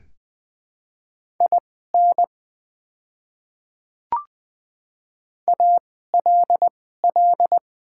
Morse code, heard as a keyed tone, about 20 wpm, spaced as for 15 wpm, 700 Hz. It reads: IN ALL
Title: 2x Character Spacing - Top 100 Words - 20wpm